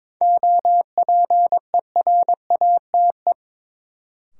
Morse code, heard as a keyed tone, about 21 wpm, 700 Hz. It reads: OPERATE
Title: Sentences from Top 1500 Words - New Word Drill - 22wpm